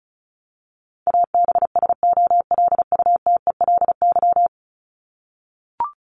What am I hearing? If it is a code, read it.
Morse code, heard as a keyed tone, about 34 wpm, 700 Hz. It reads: ABSOLUTELY
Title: Top 1501-1750 Words 35wpm